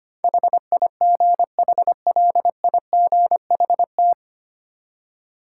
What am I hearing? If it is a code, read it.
Morse code, heard as a keyed tone, about 25 wpm, 700 Hz. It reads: HIGHLIGHT
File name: Nine Letter Words - 25wpm